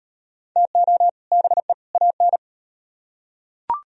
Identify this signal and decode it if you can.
Morse code, heard as a keyed tone, about 35 wpm, 700 Hz. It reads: TO BE AN